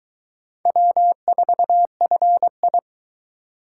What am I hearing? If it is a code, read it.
Morse code, heard as a keyed tone, about 24 wpm, 700 Hz. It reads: W4FI